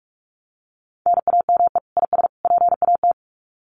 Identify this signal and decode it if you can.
Morse code, heard as a keyed tone, about 44 wpm, 700 Hz. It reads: NAME IS PAT
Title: Common QSO Phrases 45wpm